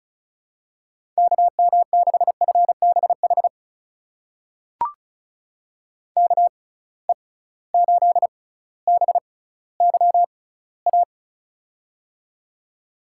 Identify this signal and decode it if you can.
Morse code, heard as a keyed tone, about 35 wpm, spaced as for 18 wpm, 690 Hz. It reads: KM6FBH KE8BYA
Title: USA Call Signs - 6 Characters - 6x Character Spacing - 35wpm